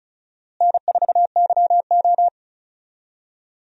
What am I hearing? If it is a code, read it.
Morse code, heard as a keyed tone, about 34 wpm, 700 Hz. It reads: N4YO
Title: USA Call Signs - 4 Characters - Repeated 3x - 35wpm